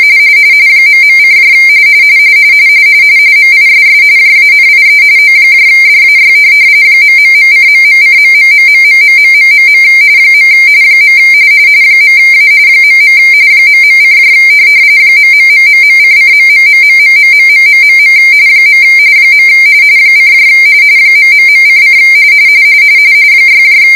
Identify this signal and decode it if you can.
RTTY, about 45.45 baud. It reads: ..: MISS JANIE ODAY ...:..:.......... . .A .. ..: .....::........... . : ...:: 6